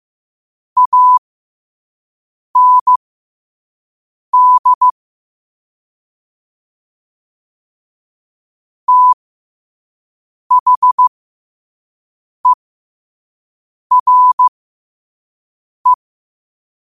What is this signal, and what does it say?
Morse code, heard as a keyed tone, about 15 wpm, spaced as for 5 wpm, 1000 Hz. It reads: AND THERE
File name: qotdfiles / 5wpm